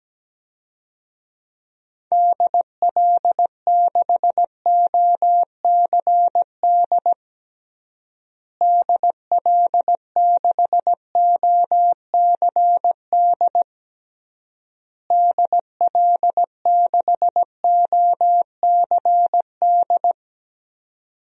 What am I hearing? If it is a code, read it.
Morse code, heard as a keyed tone, about 17 wpm, 700 Hz. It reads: DL6OCD DL6OCD DL6OCD
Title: DXCC-100 Call Signs - 6 Characters - Repeated 3x - 17wpm